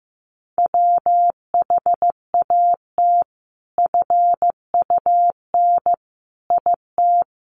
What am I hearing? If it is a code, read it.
Morse code, heard as a keyed tone, about 15 wpm, 700 Hz. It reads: WHAT FUN IT
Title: Sentences from Top 700 Words 15wpm